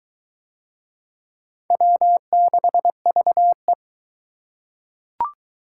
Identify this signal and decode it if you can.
Morse code, heard as a keyed tone, about 20 wpm, 700 Hz. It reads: W6VE